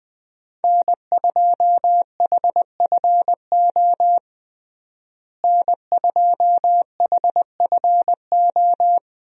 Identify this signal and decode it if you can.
Morse code, heard as a keyed tone, about 21 wpm, 700 Hz. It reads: N2HFO N2HFO
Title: USA Call Signs - 5 Characters - Repeated 2x - 20wpm